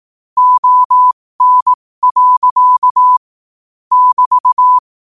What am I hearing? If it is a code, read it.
Morse code, heard as a keyed tone, about 18 wpm, spaced as for 15 wpm, 1000 Hz. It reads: ON. <BT>